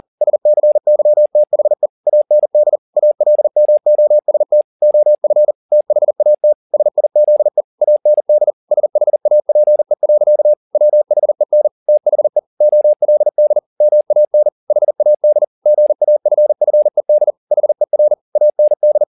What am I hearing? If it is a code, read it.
Morse code, heard as a keyed tone, about 40 wpm, 600 Hz. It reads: SCYTHE AND ALMOST OF THAT SIZE AND SHAPE. WHEN THE OLD MAN HAD GAFFED HER AND